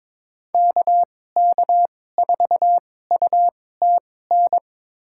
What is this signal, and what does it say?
Morse code, heard as a keyed tone, about 22 wpm, spaced as for 16 wpm, 700 Hz. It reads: KK4UTN
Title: USA Call Signs - 6 Characters - 2x Character Spacing - 22wpm